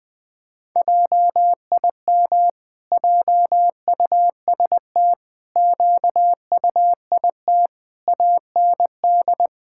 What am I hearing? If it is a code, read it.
Morse code, heard as a keyed tone, about 20 wpm, 700 Hz. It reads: JIM JUST QUIT AND
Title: Pangram Challenge v2 - 20wpm